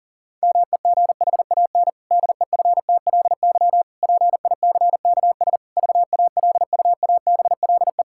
Morse code, heard as a keyed tone, about 40 wpm, 700 Hz. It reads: MEGHAN DEFTLY PICKS VALUABLE
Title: Pangram Challenge v2 - 40wpm